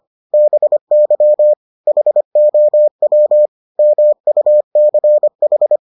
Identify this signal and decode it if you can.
Morse code, heard as a keyed tone, about 25 wpm, 600 Hz. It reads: BY HOW MUCH